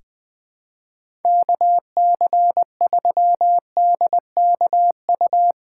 Morse code, heard as a keyed tone, about 20 wpm, 700 Hz. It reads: KC3DKU
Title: USA Callsign DE Callsign 20wpm